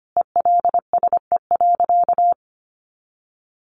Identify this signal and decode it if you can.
Morse code, heard as a keyed tone, about 25 wpm, 700 Hz. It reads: ELSE.
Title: Sentences from Top 1400 Words 25wpm